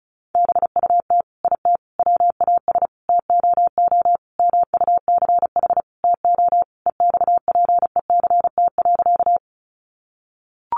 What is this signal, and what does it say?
Morse code, heard as a keyed tone, about 34 wpm, 700 Hz. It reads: BUT IT WAS TOO MUCH TO EXPECT.
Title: Sentences from Top 1100 Words 35wpm